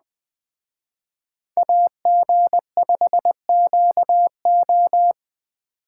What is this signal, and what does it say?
Morse code, heard as a keyed tone, about 21 wpm, 700 Hz. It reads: AG5QO